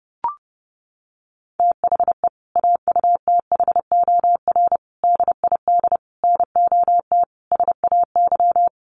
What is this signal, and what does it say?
Morse code, heard as a keyed tone, about 30 wpm, 700 Hz. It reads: THE AUTHOR DID NOT SAY
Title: Sentences from Top 1200 Words 30wpm